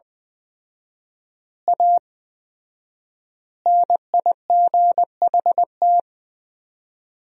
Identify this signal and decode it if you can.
Morse code, heard as a keyed tone, about 20 wpm, 700 Hz. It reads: A NIGHT
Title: 4x Word-Spacing - Sentences from Top 300 Words - 20wpm